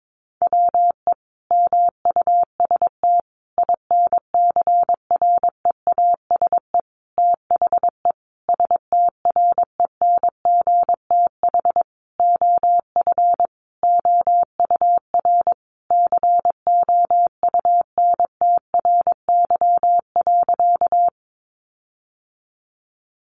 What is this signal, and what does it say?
Morse code, heard as a keyed tone, about 22 wpm, 700 Hz. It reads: WE MUST INCREASE THE STRENGTH OF OUR COUNTRY.
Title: Sentences from Top 1300 Words 22wpm